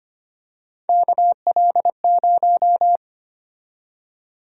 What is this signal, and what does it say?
Morse code, heard as a keyed tone, about 25 wpm, 700 Hz. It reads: KL0
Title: USA Call Sign Prefixes - Repeated 2x - 25wpm